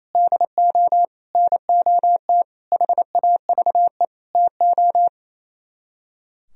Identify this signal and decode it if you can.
Morse code, heard as a keyed tone, about 35 wpm, 700 Hz. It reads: DO NOT HAVE TO